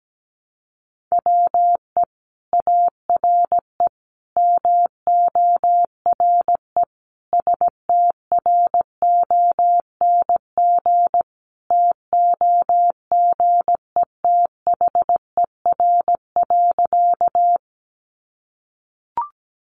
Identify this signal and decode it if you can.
Morse code, heard as a keyed tone, about 17 wpm, 700 Hz. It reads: WE ARE MORE STRONG TOGETHER.
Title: Sentences from Top 500 Words 17wpm